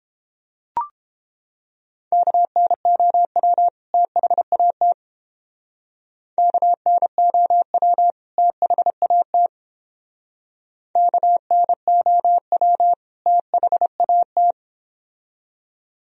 Morse code, heard as a keyed tone, about 32 wpm, 700 Hz. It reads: KNOW THAT KNOW THAT KNOW THAT